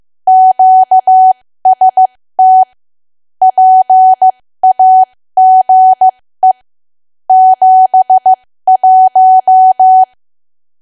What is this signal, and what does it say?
Morse code, heard as a keyed tone, about 15 wpm, spaced as for 13 wpm, 750 Hz. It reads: QST PAGE 71
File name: cwa / W1AW-Level2 / 13WPM